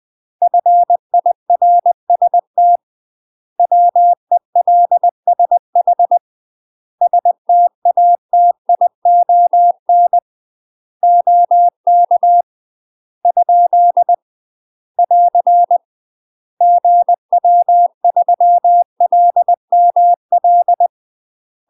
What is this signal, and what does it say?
Morse code, heard as a keyed tone, about 20 wpm, 700 Hz. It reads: FIRST WELSH STATION OK ? <AR> GW3LML
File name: UK MASTER QSOs TRAINING / 20WPM / UK MASTER QSOs 0048